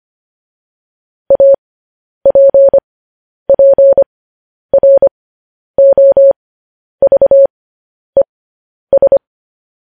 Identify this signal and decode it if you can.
Morse code, heard as a keyed tone, about 25 wpm, spaced as for 10 wpm, 550 Hz. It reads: APPROVES
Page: News Headlines in Morse Code at 10 WPM